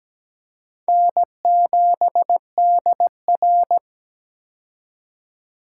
Morse code, heard as a keyed tone, about 17 wpm, 700 Hz. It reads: N7DR
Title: K1USN SST QSO Practice - 17wpm